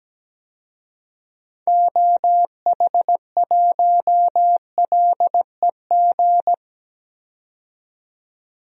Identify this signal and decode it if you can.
Morse code, heard as a keyed tone, about 17 wpm, 700 Hz. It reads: OH1LEG
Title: DXCC-100 Call Signs - 6 Characters - 17wpm